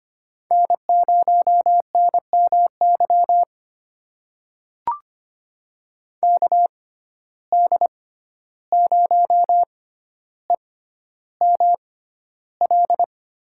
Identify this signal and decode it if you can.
Morse code, heard as a keyed tone, about 25 wpm, spaced as for 12 wpm, 700 Hz. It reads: N0NMY KD0EML
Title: USA Call Signs - 6x Character Spacing - 25wpm